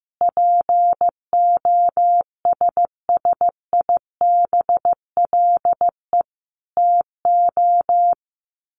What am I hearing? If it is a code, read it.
Morse code, heard as a keyed tone, about 15 wpm, 700 Hz. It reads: POSSIBLE TO
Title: Sets of 5 Words Encore2 15wpm